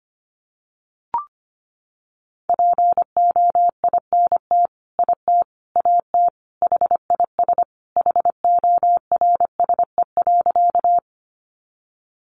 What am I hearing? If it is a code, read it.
Morse code, heard as a keyed tone, about 25 wpm, 700 Hz. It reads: POINT IT AT HIS HORSE.
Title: Sentences from Top 300 Words 25wpm